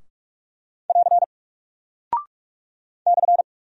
Morse code, heard as a keyed tone, about 44 wpm, 700 Hz. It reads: <AR> /
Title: Single Character (Letter:Number:Punctuation:Prosign) Alt 45wpm